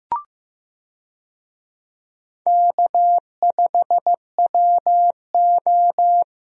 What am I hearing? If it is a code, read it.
Morse code, heard as a keyed tone, about 15 wpm, 700 Hz. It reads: K5WO
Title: USA Call Signs - 4 Characters - 15wpm